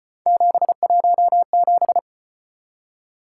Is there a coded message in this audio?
Yes: Morse code from 0.3 to 2.0 s, heard as a keyed tone, about 34 wpm, 700 Hz. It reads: 717